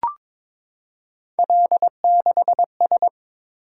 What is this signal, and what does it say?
Morse code, heard as a keyed tone, about 21 wpm, 700 Hz. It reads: L6S
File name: practice / 22wpm / Three Letter-Number Combinations - 22wpm